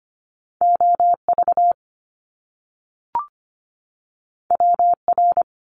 Morse code, heard as a keyed tone, about 25 wpm, 700 Hz. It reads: OV WR
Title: Top Two Letter Combinations in Words 25wpm